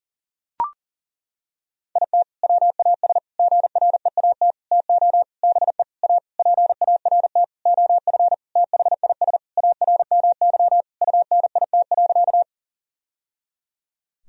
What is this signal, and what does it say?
Morse code, heard as a keyed tone, about 40 wpm, 700 Hz. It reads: IT WAS GREAT TO BE A PART OF THIS ARMY UNIT.
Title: Sentences from Top 1000 Words - Review Part 2 - 40wpm